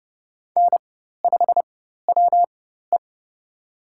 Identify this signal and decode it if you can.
Morse code, heard as a keyed tone, about 30 wpm, spaced as for 14 wpm, 700 Hz. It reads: N5WE
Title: USA Call Signs - 4x Character Spacing - 30wpm